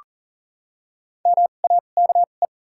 Morse code, heard as a keyed tone, about 40 wpm, spaced as for 29 wpm, 700 Hz. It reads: MAKE
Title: 2x Character Spacing - Top 100 Words - 40wpm